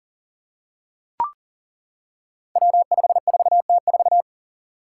Morse code, heard as a keyed tone, about 42 wpm, 700 Hz. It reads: W54T4
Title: Five Challenging Letter-Number Combinations 40wpm